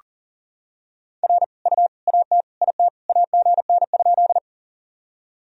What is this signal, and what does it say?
Morse code, heard as a keyed tone, about 40 wpm, 700 Hz. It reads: R U AT IT AGN?
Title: Sentences from Top 200 Words and Abbreviations 40wpm